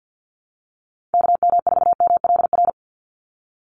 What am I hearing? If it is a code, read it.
Morse code, heard as a keyed tone, about 50 wpm, 700 Hz. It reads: KM4MLR
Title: USA Call Signs 50wpm